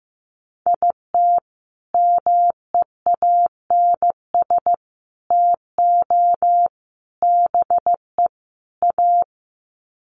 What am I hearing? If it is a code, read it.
Morse code, heard as a keyed tone, about 15 wpm, 700 Hz. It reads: IT MEANS TO BE A
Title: Sets of 5 Words Encore2 15wpm